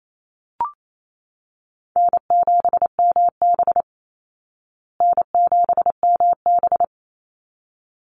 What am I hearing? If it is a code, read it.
Morse code, heard as a keyed tone, about 28 wpm, 700 Hz. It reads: N7MB N7MB
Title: Easy Call Signs Repeated 2x - 28wpm